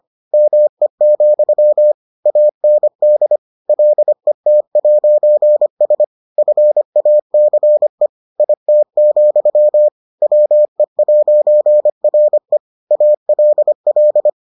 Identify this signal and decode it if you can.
Morse code, heard as a keyed tone, about 25 wpm, 600 Hz. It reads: ME, AND LET'S FACE IT, WE'RE ALL